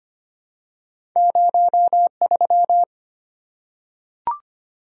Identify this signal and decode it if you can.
Morse code, heard as a keyed tone, about 25 wpm, 700 Hz. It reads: 03
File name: Two Letter-Number Combinations - 25wpm